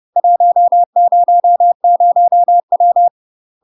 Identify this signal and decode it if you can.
Morse code, heard as a keyed tone, about 30 wpm, 700 Hz. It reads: 100W
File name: UK MASTER QSOs TRAINING / 30WPM / UK MASTER QSOs 0014